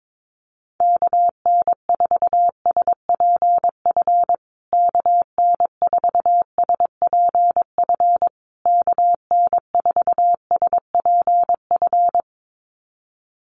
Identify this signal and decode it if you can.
Morse code, heard as a keyed tone, about 22 wpm, 700 Hz. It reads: KN4SPF KN4SPF KN4SPF
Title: USA Call Signs Repeated 3x 22wpm